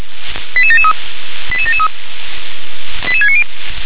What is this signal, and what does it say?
Five-tone selective call (ZVEI), heard as quick runs of five tones, tones of about 70 ms. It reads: 88773 88773 88690
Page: ZVEI